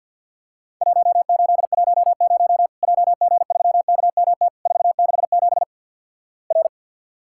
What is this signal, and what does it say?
Morse code, heard as a keyed tone, about 50 wpm, 700 Hz. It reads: 1810 JG3KMT 467 R
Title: Worked All Europe DX Contest - QTC Exchange with Repeats - 50wpm